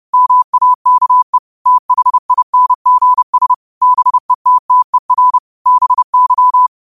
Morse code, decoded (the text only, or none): MAKE THINGS BETTER BY